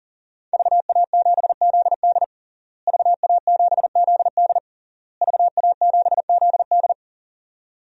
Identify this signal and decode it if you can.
Morse code, heard as a keyed tone, about 37 wpm, 700 Hz. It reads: VA7ZD VA7ZD VA7ZD
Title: DXCC-100 Call Signs - 5 Characters - Repeated 3x - 40wpm